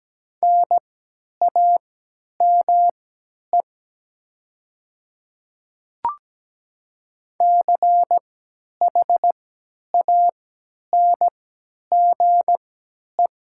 Morse code, heard as a keyed tone, about 17 wpm, spaced as for 10 wpm, 700 Hz. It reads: NAME CHANGE
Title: 3x Character Spacing - Top 200 Words - 17wpm